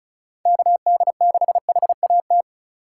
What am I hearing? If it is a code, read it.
Morse code, heard as a keyed tone, about 34 wpm, 700 Hz. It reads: KD6HAT
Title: USA Call Signs - 4x Character Spacing - 35wpm